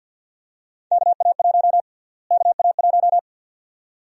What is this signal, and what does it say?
Morse code, heard as a keyed tone, about 52 wpm, 700 Hz. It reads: KA1 KA1
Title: USA Call Sign Prefixes - Repeated 2x - 50wpm